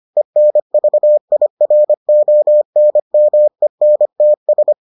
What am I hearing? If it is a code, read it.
Morse code, heard as a keyed tone, about 25 wpm, 600 Hz. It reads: ENVIRONMENTS